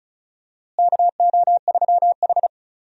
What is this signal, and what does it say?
Morse code, heard as a keyed tone, about 36 wpm, 700 Hz. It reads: KO3H
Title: USA Call Signs - 4 Characters - Repeated 3x - 35wpm